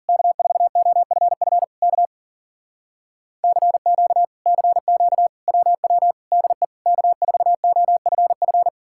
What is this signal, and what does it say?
Morse code, heard as a keyed tone, about 52 wpm, 700 Hz. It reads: K4OFF K CQ CQ WW DE K4OFF